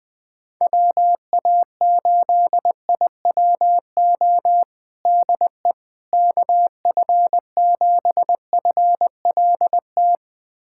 Morse code, heard as a keyed tone, about 20 wpm, 700 Hz. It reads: WA8IWO DE KF7FLT
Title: USA Callsign DE Callsign 20wpm